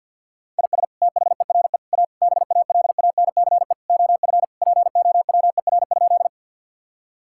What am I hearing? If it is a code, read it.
Morse code, heard as a keyed tone, about 50 wpm, 700 Hz. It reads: IS THERE A BALANCE OF POWER?
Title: Sentences from Top 1000 Words - Review Part 6 - 50wpm